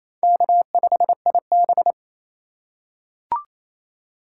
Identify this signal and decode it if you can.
Morse code, heard as a keyed tone, about 28 wpm, 700 Hz. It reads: K5IB
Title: USA Call Signs - 4 Characters - 2x Character Spacing - 28wpm